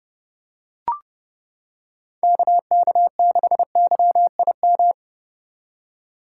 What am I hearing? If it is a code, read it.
Morse code, heard as a keyed tone, about 29 wpm, 700 Hz. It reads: KK6YIM